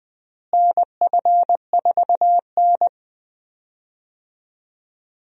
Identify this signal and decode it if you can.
Morse code, heard as a keyed tone, about 20 wpm, 700 Hz. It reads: NF4N
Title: USA Call Signs - 4 Characters - Repeated 2x - 20wpm